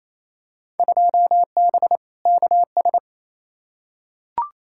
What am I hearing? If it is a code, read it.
Morse code, heard as a keyed tone, about 28 wpm, 700 Hz. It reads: 2B KS
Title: Field Day Exchanges Encore - 28wpm